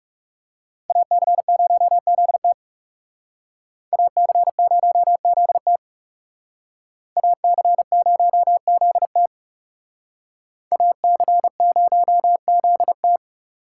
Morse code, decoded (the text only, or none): AC0ZT AC0ZT AC0ZT AC0ZT